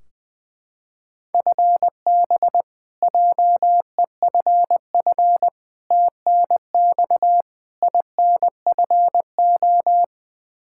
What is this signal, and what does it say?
Morse code, heard as a keyed tone, about 21 wpm, 700 Hz. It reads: FB JEFF TNX INFO
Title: Practice QSO - Exchange 3 - More Info - 20wpm